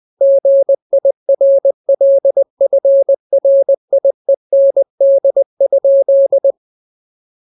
Morse code, heard as a keyed tone, about 20 wpm, 550 Hz. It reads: GIRLFRIEND?